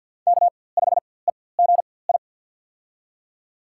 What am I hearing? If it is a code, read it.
Morse code, heard as a keyed tone, about 50 wpm, spaced as for 23 wpm, 700 Hz. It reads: K5EGI
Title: USA Call Signs - 5 Characters - 4x Character Spacing - 50wpm